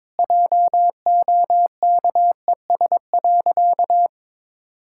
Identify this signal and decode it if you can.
Morse code, heard as a keyed tone, about 22 wpm, 700 Hz. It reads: JOKES.
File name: practice / 22wpm / Easy Sentences with Hints - 22wpm